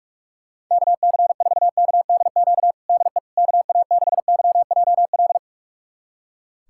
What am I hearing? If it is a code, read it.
Morse code, heard as a keyed tone, about 42 wpm, 700 Hz. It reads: KC4KDQ DE KA6YJL
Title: USA Callsign DE Callsign - 6 Characters - 45wpm